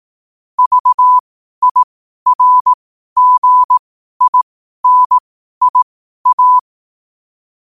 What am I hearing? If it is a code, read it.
Morse code, heard as a keyed tone, about 18 wpm, spaced as for 12 wpm, 1000 Hz. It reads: VIRGINIA
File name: qotdfiles / 12wpm